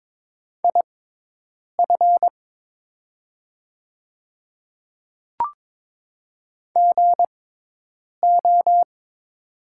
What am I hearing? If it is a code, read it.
Morse code, heard as a keyed tone, about 22 wpm, spaced as for 8 wpm, 690 Hz. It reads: IF GO
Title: 6x Character Spacing - Top 100 Words - 22wpm